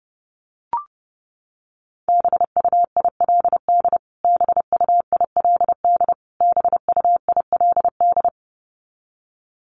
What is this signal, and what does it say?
Morse code, heard as a keyed tone, about 30 wpm, 700 Hz. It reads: BUILD BUILD BUILD
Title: Top 200 Words Repeated 30wpm